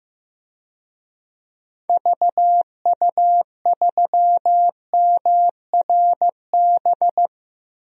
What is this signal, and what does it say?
Morse code, heard as a keyed tone, about 15 wpm, 700 Hz. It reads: VU3MRB
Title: DXCC-100 Call Signs - 6 Characters - 15wpm